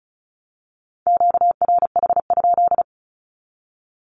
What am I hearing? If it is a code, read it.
Morse code, heard as a keyed tone, about 34 wpm, 700 Hz. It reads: QRH?
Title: Q-Codes 35wpm